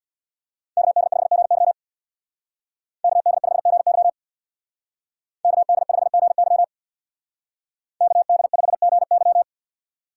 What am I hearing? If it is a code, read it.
Morse code, heard as a keyed tone, about 64 wpm, 700 Hz. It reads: KD5GY KD5GY KD5GY KD5GY